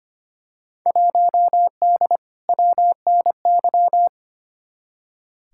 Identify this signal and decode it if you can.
Morse code, heard as a keyed tone, about 25 wpm, 700 Hz. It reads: 1D WNY
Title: Field Day Exchanges - 25wpm